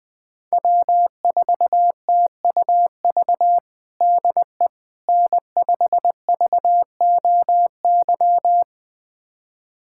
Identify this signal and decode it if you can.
Morse code, heard as a keyed tone, about 21 wpm, 700 Hz. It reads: W4TUV DE N5VOY
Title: USA Callsign DE Callsign - 5 Characters - 20wpm